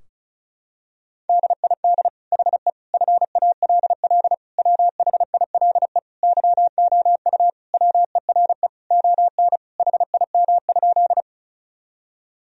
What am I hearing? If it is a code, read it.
Morse code, heard as a keyed tone, about 34 wpm, 700 Hz. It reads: DID HE FALL WHILE YOU WERE ON HIM?
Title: Sentences from Top 1000 Words - Review Part 2 - 35wpm